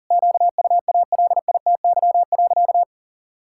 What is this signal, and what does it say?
Morse code, heard as a keyed tone, about 40 wpm, 700 Hz. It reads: QUALITY.
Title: Sentences from Top 1000 Words - Review Part 5 - 40wpm